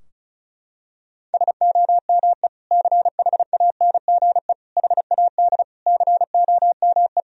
Morse code, heard as a keyed tone, about 36 wpm, 700 Hz. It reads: SOME CHANGE HAD COME
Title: Sentences from Top 1000 Words - Review Part 2 - 35wpm